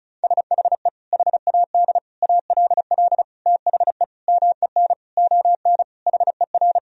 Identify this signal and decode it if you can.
Morse code, heard as a keyed tone, about 35 wpm, 700 Hz. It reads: SHE HAD ALL THE MEN ON HER
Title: Sentences from Top 1000 Words - Review Part 1 - 35wpm